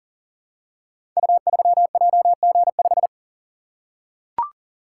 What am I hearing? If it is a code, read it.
Morse code, heard as a keyed tone, about 44 wpm, 700 Hz. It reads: U3JG5
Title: Code Groups - 5 Letter:Numbers - 40wpm